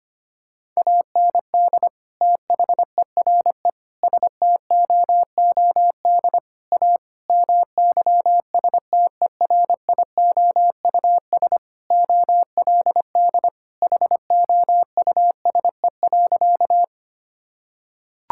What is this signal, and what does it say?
Morse code, heard as a keyed tone, about 25 wpm, 700 Hz. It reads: AND THERE STOOD A MYSTERIOUS OLD HOUSE.
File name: practice / 25wpm / Sentences with 2-Word Phrases - 25wpm